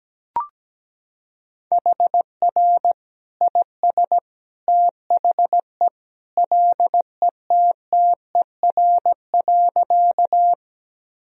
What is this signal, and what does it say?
Morse code, heard as a keyed tone, about 17 wpm, 700 Hz. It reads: HR IS THE LETTER.
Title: Sentences from Top 300 Words and Abbreviations 17wpm